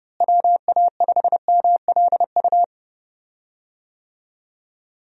Morse code, heard as a keyed tone, about 30 wpm, 700 Hz. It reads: WA5MLU
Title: USA Call Signs - 6 Characters - Repeated 3x - 30wpm